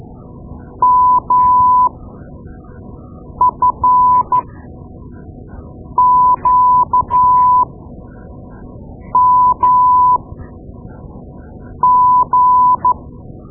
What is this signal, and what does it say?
Morse code, heard as a keyed tone, about 13 wpm, spaced as for 5 wpm, 1000 Hz. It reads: MFQMG
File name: Morse Lessons / Morse Lessons / Tape-2B